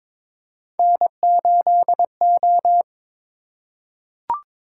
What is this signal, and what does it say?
Morse code, heard as a keyed tone, about 23 wpm, 700 Hz. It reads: N8O
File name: practice / 22wpm / Three Letter-Number Combinations - 22wpm